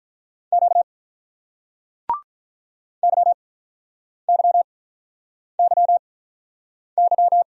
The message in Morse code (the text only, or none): Q YYYY